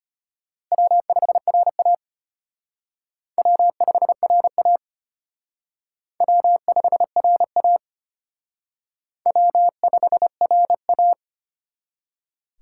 Morse code, heard as a keyed tone, about 34 wpm, 700 Hz. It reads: W5RA W5RA W5RA W5RA